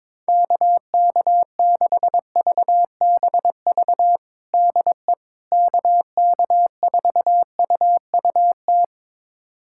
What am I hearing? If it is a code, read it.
Morse code, heard as a keyed tone, about 22 wpm, 700 Hz. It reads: KK6VBV DE KK4UUT
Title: USA Callsign DE Callsign - 6 Characters - 22wpm